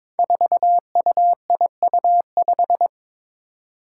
Morse code, heard as a keyed tone, about 22 wpm, 700 Hz. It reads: 4UIU5
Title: Five Challenging Letter-Number Combinations 22wpm